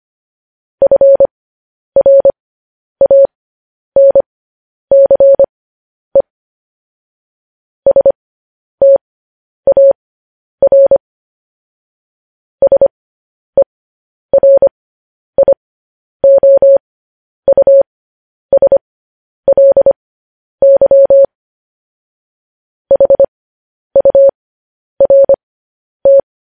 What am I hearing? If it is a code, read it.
Morse code, heard as a keyed tone, about 25 wpm, spaced as for 10 wpm, 550 Hz. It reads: FRANCE STAR SERIOUSLY HURT